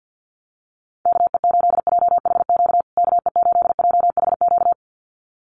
Morse code, heard as a keyed tone, about 50 wpm, 700 Hz. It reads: KE8JHQ KE8JHQ
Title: USA Call Signs Repeated 2x 50wpm